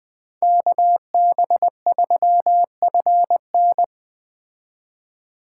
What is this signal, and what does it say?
Morse code, heard as a keyed tone, about 20 wpm, 700 Hz. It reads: KB3FN
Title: Speed Racing - USA 2x2 Call Signs 17wpm